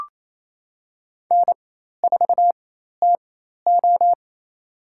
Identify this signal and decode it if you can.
Morse code, heard as a keyed tone, about 28 wpm, spaced as for 13 wpm, 700 Hz. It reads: N4TO
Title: USA Call Signs - 4 Characters - 4x Character Spacing - 28wpm